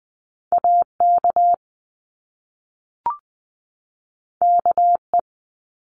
Morse code, heard as a keyed tone, about 20 wpm, 700 Hz. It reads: AK KE
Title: Top Two Letter Combinations in Words 20wpm